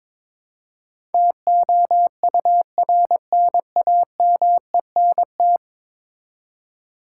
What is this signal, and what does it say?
Morse code, heard as a keyed tone, about 22 wpm, 700 Hz. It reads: TOURNAMENT
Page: morse-code-ninja-practice-22wpm